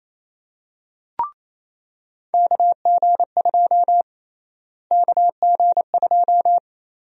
Morse code, heard as a keyed tone, about 28 wpm, 700 Hz. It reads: KG2 KG2
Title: USA Call Sign Prefixes - Repeated 2x - 28wpm